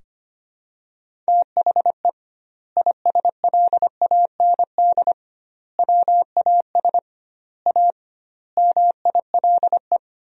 Morse code, heard as a keyed tone, about 25 wpm, 700 Hz. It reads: THE ISLAND WAS A MILE
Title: 2x Word-Spacing - Sentences from Top 500 Words - 25wpm